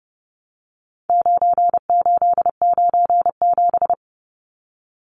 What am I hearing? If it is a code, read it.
Morse code, heard as a keyed tone, about 30 wpm, 700 Hz. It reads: 9897
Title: Numbers - 4-Digit 30wpm